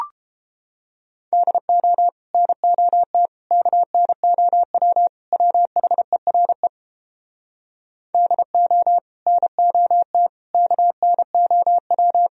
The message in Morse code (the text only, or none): DO NOT KNOW WHERE DO NOT KNOW